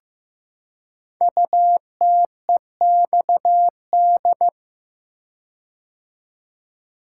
Morse code, heard as a keyed tone, about 15 wpm, 690 Hz. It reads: UTEXD
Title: Code Groups - 5 Letters - 15wpm